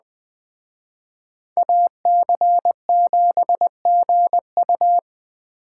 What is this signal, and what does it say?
Morse code, heard as a keyed tone, about 20 wpm, 700 Hz. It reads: AC7GU
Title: Speed Racing - USA Call Signs 17wpm